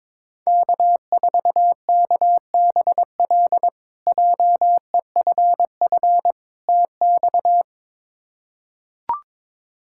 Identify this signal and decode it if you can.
Morse code, heard as a keyed tone, about 23 wpm, 700 Hz. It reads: K4KBL JEFF TX
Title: K1USN SST QSO Practice - 22wpm